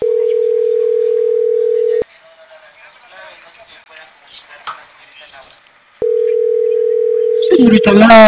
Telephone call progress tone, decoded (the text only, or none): ringback tone